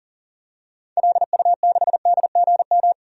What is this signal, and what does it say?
Morse code, heard as a keyed tone, about 40 wpm, 700 Hz. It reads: LU6DGM
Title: DXCC-100 Call Signs - 6 Characters - 40wpm